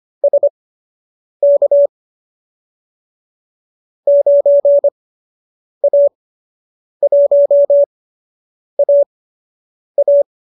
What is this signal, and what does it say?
Morse code, heard as a keyed tone, about 25 wpm, spaced as for 8 wpm, 570 Hz. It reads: SK 9A1AA